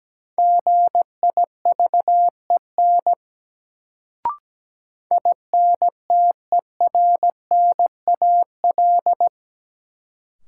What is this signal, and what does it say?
Morse code, heard as a keyed tone, about 17 wpm, 700 Hz. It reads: GIVEN INTERNAL
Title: Sentences from Top 1400 Words - New Word Drill Alt - 17wpm